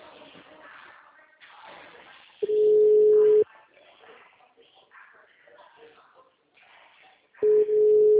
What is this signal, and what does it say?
Ringback tone (Europe, 425 Hz), with speech.
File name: out- -2014- - - .111